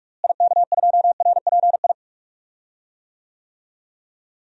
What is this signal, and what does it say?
Morse code, heard as a keyed tone, about 46 wpm, 700 Hz. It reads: IK2RPI
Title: DXCC-100 Call Signs - 6 Characters - Repeated 2x - 45wpm